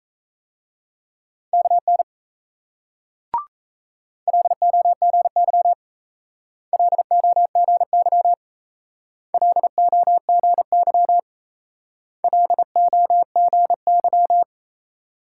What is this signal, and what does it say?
Morse code, heard as a keyed tone, about 36 wpm, 700 Hz. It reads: KN LOGY LOGY LOGY LOGY